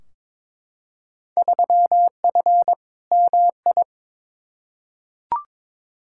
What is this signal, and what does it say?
Morse code, heard as a keyed tone, about 22 wpm, 700 Hz. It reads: 3F MI
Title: Field Day Exchanges - 22wpm